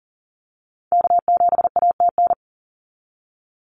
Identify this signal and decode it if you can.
Morse code, heard as a keyed tone, about 39 wpm, 700 Hz. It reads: K7ATN
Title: NA SOTA Activator Callsigns 40wpm